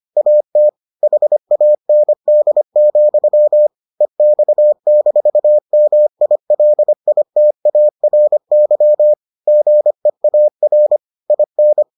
Morse code, heard as a keyed tone, about 25 wpm, 600 Hz. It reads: AT HAND, EX-MILITARY GEAR IN